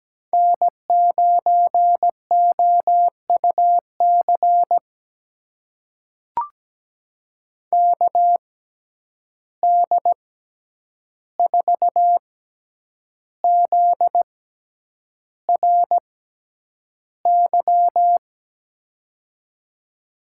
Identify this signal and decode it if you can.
Morse code, heard as a keyed tone, about 17 wpm, spaced as for 8 wpm, 700 Hz. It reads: N9OUC KD4ZRY
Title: USA Call Signs - 6x Character Spacing - 17wpm